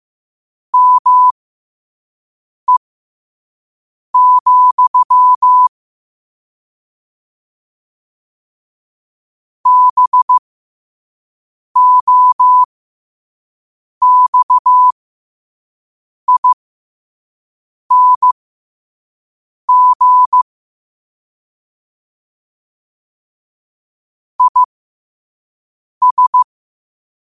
Morse code, heard as a keyed tone, about 15 wpm, spaced as for 5 wpm, 1000 Hz. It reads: ME, BOXING IS
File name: qotdfiles / 5wpm